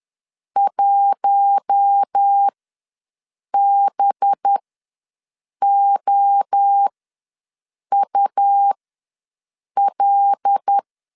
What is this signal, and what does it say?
Morse code, heard as a keyed tone, about 10 wpm, spaced as for 6 wpm, 800 Hz. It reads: 1BOUL
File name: LEZIONI / 5 LBVC / 5 LB ws 10wpm 3616